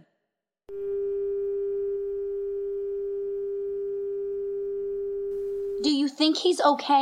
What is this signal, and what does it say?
Dial tone (Japan), 400 Hz, with speech.